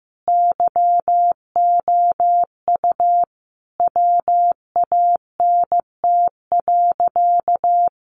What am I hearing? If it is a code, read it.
Morse code, heard as a keyed tone, about 15 wpm, 700 Hz. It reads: YOU WANT.
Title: Sentences from Top 1300 Words 15wpm